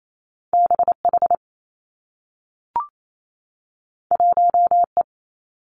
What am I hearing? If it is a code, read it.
Morse code, heard as a keyed tone, about 28 wpm, 700 Hz. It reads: B H 1 E
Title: Two Challenging Letter-Number Combinations 28wpm